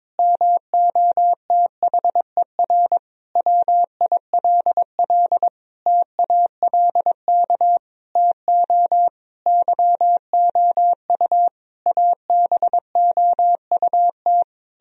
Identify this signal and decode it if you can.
Morse code, heard as a keyed tone, about 22 wpm, 700 Hz. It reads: MOTHER WILL TALK TO YOU ABOUT
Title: Sentences from Top 1000 Words - Review Part 1 - 22wpm